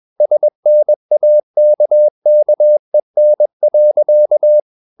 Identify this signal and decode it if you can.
Morse code, heard as a keyed tone, about 21 wpm, 600 Hz. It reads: SNAKKEN.
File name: Ferie QRS 21wpm